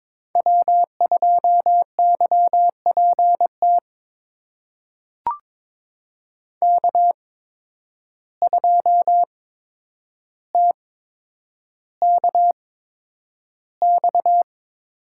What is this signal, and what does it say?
Morse code, heard as a keyed tone, about 22 wpm, spaced as for 9 wpm, 700 Hz. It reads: W2YPT K2TKX
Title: USA Call Signs - 5 Characters - 8x Character Spacing - 22wpm